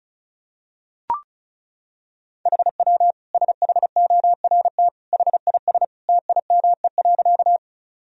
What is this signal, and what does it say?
Morse code, heard as a keyed tone, about 36 wpm, 700 Hz. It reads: HW SHORT HIS TIME.
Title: Sentences from Top 400 Words and Abbreviations 35wpm